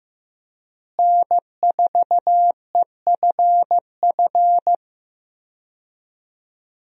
Morse code, heard as a keyed tone, about 15 wpm, 700 Hz. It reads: N4EFF
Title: USA Call Signs - 5 Characters - Repeated 3x - 15wpm